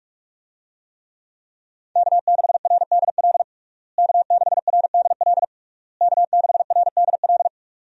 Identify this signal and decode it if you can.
Morse code, heard as a keyed tone, about 42 wpm, 700 Hz. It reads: K6RDL K6RDL K6RDL
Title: USA Call Signs - 5 Characters - Repeated 3x - 45wpm